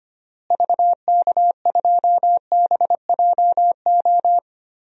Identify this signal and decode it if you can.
Morse code, heard as a keyed tone, about 25 wpm, 700 Hz. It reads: VK2BJO
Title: DXCC-100 Call Signs - 6 Characters - Repeated 2x - 25wpm